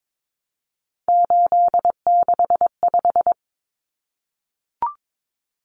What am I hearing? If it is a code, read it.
Morse code, heard as a keyed tone, about 22 wpm, 700 Hz. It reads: 865
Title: Common Cut Numbers - 3-Digit 22wpm